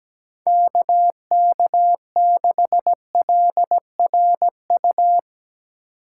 Morse code, heard as a keyed tone, about 17 wpm, 700 Hz. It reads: KK6LRU
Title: USA Call Signs - 6 Characters - 3x Character Spacing - 17wpm